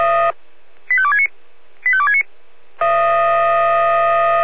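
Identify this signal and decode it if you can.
Five-tone selective call (ZVEI), heard as quick runs of five tones, tones of about 70 ms. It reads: 85269 85269
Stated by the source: ZVEI five-tone selective call received by SDR.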